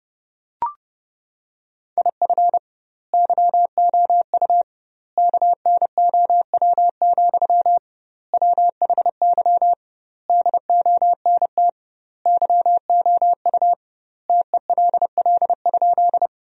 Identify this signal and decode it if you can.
Morse code, heard as a keyed tone, about 32 wpm, 700 Hz. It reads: IF YOU KNOW, WHY DONT YOU TELL?
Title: 2x Word-Spacing - Sentences from Top 300 Words - 30wpm